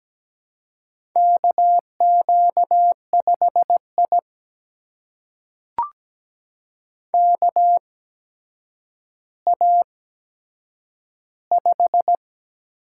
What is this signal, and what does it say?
Morse code, heard as a keyed tone, about 17 wpm, spaced as for 8 wpm, 700 Hz. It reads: KQ5I KA5
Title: USA Call Signs - 4 Characters - 8x Character Spacing - 17wpm